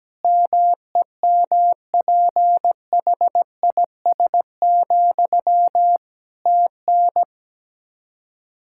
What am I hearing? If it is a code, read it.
Morse code, heard as a keyed tone, about 17 wpm, 700 Hz. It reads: MEMPHIS, TN
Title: Practice QSO - Exchange 1 - Beginning - 17wpm